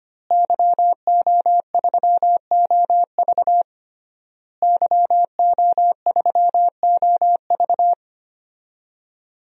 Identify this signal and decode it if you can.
Morse code, heard as a keyed tone, about 25 wpm, 700 Hz. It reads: YO3OV YO3OV
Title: DXCC-100 Call Signs - 5 Characters - Repeated 3x - 25wpm